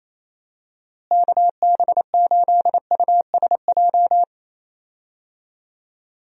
Morse code, heard as a keyed tone, about 28 wpm, 700 Hz. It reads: KB8USJ
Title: USA Call Signs - 6 Characters - Repeated 3x - 28wpm